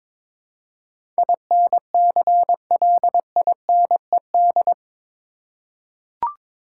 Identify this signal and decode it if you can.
Morse code, heard as a keyed tone, about 22 wpm, 700 Hz. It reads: INCLINED
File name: practice / 22wpm / Eight Letter Words - 22wpm